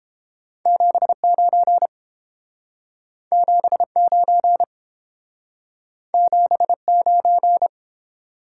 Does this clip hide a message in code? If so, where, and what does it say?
Yes: Morse code from 0.7 to 7.7 s, heard as a keyed tone, about 30 wpm, 700 Hz. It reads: 79 79 79